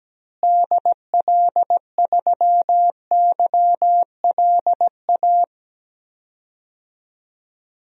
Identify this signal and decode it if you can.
Morse code, heard as a keyed tone, about 17 wpm, 700 Hz. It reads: DL3YLA
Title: DXCC-100 Call Signs - 6 Characters - Repeated 2x - 17wpm